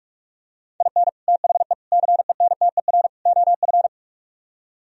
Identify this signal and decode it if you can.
Morse code, heard as a keyed tone, about 46 wpm, 700 Hz. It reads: IN THE CENTER OF